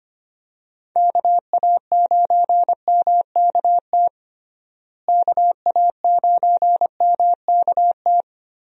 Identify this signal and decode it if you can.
Morse code, heard as a keyed tone, about 25 wpm, 700 Hz. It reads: KA9MKT KA9MKT
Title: USA Call Signs - 6 Characters - Repeated 2x - 25wpm